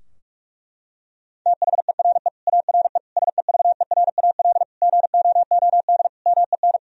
Morse code, heard as a keyed tone, about 46 wpm, 700 Hz. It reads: THERE ARE SEVERAL GOOD MEN
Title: Sentences from Top 1000 Words - Review Part 2 - 45wpm